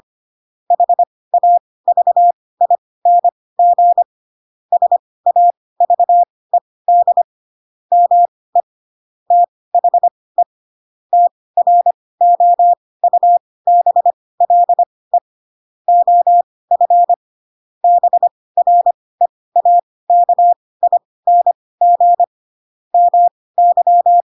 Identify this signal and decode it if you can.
Morse code, heard as a keyed tone, about 25 wpm, spaced as for 18 wpm, 700 Hz. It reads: HAVING SAVED ME THE TROUBLE OF BREAKING MY
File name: Pinocchio / Pinocchio ch5 12 25 ik1hge